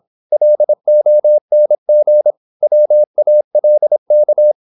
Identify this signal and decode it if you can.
Morse code, heard as a keyed tone, about 26 wpm, 600 Hz. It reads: LONG WALK
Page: Never 26wpm | CW med Gnister